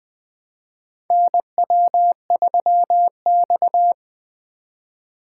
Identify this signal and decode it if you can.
Morse code, heard as a keyed tone, about 21 wpm, 700 Hz. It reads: NW3X
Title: USA Call Signs - 2x Character Spacing - 20wpm